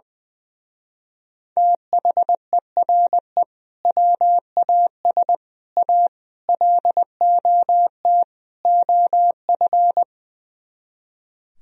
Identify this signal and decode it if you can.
Morse code, heard as a keyed tone, about 20 wpm, 700 Hz. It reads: THERE WAS A LOT OF
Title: Speed-Racing - Sets of 5 Words 20wpm